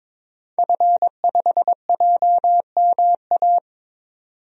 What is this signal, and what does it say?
Morse code, heard as a keyed tone, about 22 wpm, 700 Hz. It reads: F5JMA
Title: DXCC-100 Call Signs - 5 Characters - 22wpm